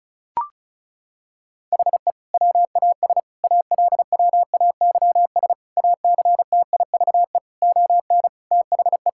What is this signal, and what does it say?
Morse code, heard as a keyed tone, about 36 wpm, 700 Hz. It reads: HE WAS ALWAYS ACTIVE ON THE
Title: Sentences from Top 1000 Words - Review Part 6 - 35wpm